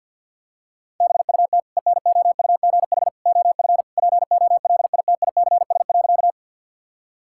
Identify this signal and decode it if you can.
Morse code, heard as a keyed tone, about 52 wpm, 700 Hz. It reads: BUT ENOUGH OF POLITICS.
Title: Sentences from Top 1000 Words - Review Part 6 - 50wpm